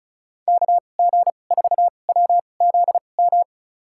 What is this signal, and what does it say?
Morse code, heard as a keyed tone, about 36 wpm, spaced as for 25 wpm, 700 Hz. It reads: KG4WZM
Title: USA Call Signs - 6 Characters - 2x Character Spacing - 35wpm